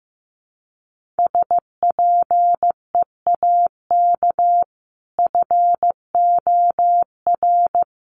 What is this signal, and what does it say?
Morse code, heard as a keyed tone, about 15 wpm, 700 Hz. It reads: SPEAK FOR
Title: Sentences from Top 1100 Words 15wpm